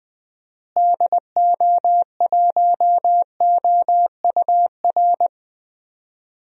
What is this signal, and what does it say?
Morse code, heard as a keyed tone, about 20 wpm, 700 Hz. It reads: DO1OUR
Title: DXCC-100 Call Signs - 6 Characters - Repeated 2x - 20wpm